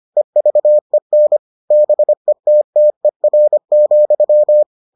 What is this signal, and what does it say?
Morse code, heard as a keyed tone, about 25 wpm, 600 Hz. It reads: EVEN BETTER,